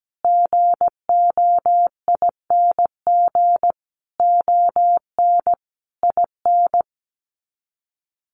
Morse code, heard as a keyed tone, about 17 wpm, 700 Hz. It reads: GOING ON IN
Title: Sets of 5 Words Encore 17wpm